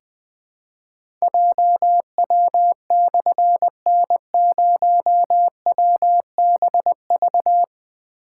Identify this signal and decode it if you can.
Morse code, heard as a keyed tone, about 20 wpm, 700 Hz. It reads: JW/N0WBV
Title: Difficult Call Signs 20wpm